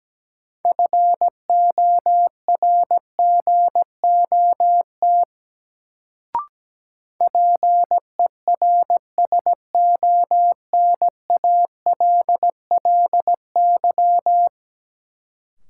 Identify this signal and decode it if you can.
Morse code, heard as a keyed tone, about 17 wpm, 700 Hz. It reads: FORGOT PERSONALLY
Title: Sentences from Top 1400 Words - New Word Drill - 17wpm